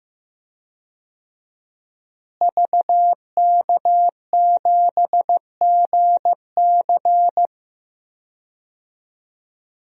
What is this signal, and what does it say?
Morse code, heard as a keyed tone, about 15 wpm, 700 Hz. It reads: VK7GC
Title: DXCC-100 Call Signs - 5 Characters - Repeated 3x - 15wpm